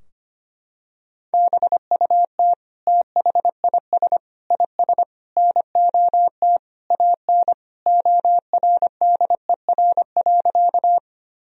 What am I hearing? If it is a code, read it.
Morse code, heard as a keyed tone, about 25 wpm, 700 Hz. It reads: BUT THIS IS NOT AN ORDER.
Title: Sentences from Top 1000 Words - Review Part 2 - 25wpm